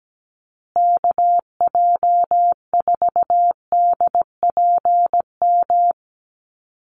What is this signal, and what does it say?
Morse code, heard as a keyed tone, about 17 wpm, 700 Hz. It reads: KJ4DPM
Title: USA Call Signs Repeated 2x 17wpm